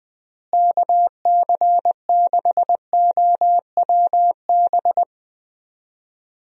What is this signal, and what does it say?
Morse code, heard as a keyed tone, about 20 wpm, 700 Hz. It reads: KC6OWB
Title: USA Call Signs - 6 Characters - Repeated 3x - 20wpm